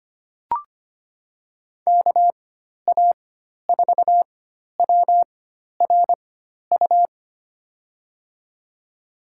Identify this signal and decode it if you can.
Morse code, heard as a keyed tone, about 25 wpm, spaced as for 12 wpm, 700 Hz. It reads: KA4WRU
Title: USA Call Signs - 6 Characters - 4x Character Spacing - 25wpm